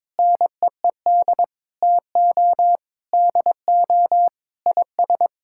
Morse code, heard as a keyed tone, about 22 wpm, 700 Hz. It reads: NEED TO DO IS
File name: practice / 22wpm / Sentences with 8-Word Phrases - 22wpm